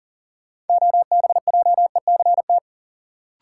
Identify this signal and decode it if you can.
Morse code, heard as a keyed tone, about 40 wpm, 700 Hz. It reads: OBJECT
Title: Sentences from Top 500 Words - New Word Drill - 40wpm